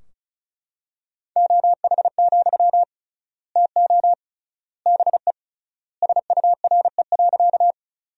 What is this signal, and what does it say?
Morse code, heard as a keyed tone, about 36 wpm, 700 Hz. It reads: OH, TO BE SURE.
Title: 3x Word-Spacing - Sentences from Top 500 Words - 35wpm